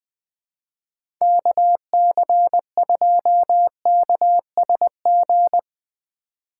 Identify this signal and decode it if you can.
Morse code, heard as a keyed tone, about 20 wpm, 700 Hz. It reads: KC2KSG
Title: USA Call Signs - 6 Characters - 8x Character Spacing - 20wpm